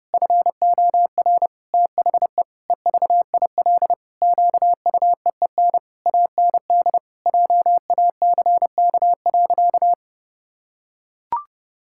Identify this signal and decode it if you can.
Morse code, heard as a keyed tone, about 30 wpm, 700 Hz. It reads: FOR THE EVIL QUEEN AND JACK.
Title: Pangram Challenge v2 - 30wpm